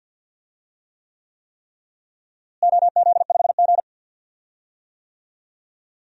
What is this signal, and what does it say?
Morse code, heard as a keyed tone, about 47 wpm, 700 Hz. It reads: OZ5G
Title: DXCC-100 Call Signs - 4 Characters - Repeated 3x - 50wpm